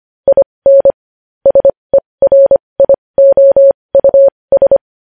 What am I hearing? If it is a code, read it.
Morse code, heard as a keyed tone, about 25 wpm, spaced as for 20 wpm, 550 Hz. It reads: IN SERIOUS
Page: news headlines in morse code at 20 wpm